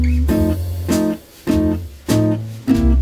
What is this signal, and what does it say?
There is no code.